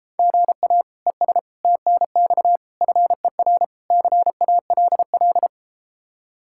Morse code, heard as a keyed tone, about 33 wpm, 700 Hz. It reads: GA ES TNX FER CALL